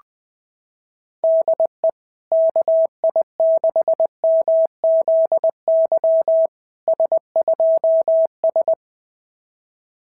Morse code, heard as a keyed tone, about 20 wpm, 690 Hz. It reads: DE KI6MZY S2S
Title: SOTA QSO Practice 20wpm